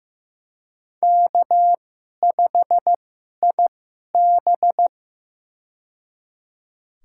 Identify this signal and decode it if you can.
Morse code, heard as a keyed tone, about 15 wpm, spaced as for 11 wpm, 700 Hz. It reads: K5IB
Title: USA Call Signs - 4 Characters - 2x Character Spacing - 15wpm